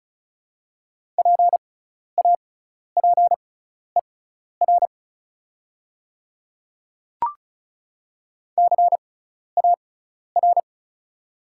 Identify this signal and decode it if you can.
Morse code, heard as a keyed tone, about 36 wpm, spaced as for 12 wpm, 690 Hz. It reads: PAPER CAR
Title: 6x Character Spacing - Top 300 Words - 35wpm